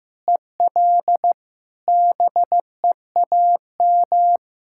Morse code, heard as a keyed tone, about 15 wpm, 700 Hz. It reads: EL BEAM
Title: K1USN SST QSO Practice 15wpm